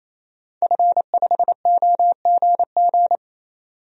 Morse code, heard as a keyed tone, about 28 wpm, 700 Hz. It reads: F5OGG
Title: DXCC-100 Call Signs - 5 Characters - 28wpm